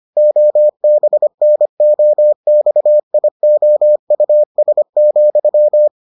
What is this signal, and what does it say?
Morse code, heard as a keyed tone, about 25 wpm, 600 Hz. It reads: OBNOXIOUS,